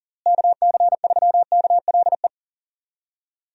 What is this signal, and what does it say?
Morse code, heard as a keyed tone, about 40 wpm, 700 Hz. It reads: KC3KLE
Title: USA Call Signs - 6 Characters - Repeated 3x - 40wpm